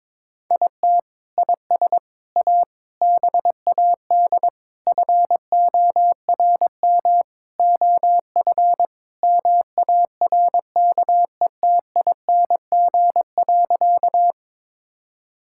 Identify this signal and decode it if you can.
Morse code, heard as a keyed tone, about 21 wpm, 700 Hz. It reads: IT IS A BAD FORM OF MARKETING.
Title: Sentences from Top 1000 Words - Review Part 5 - 22wpm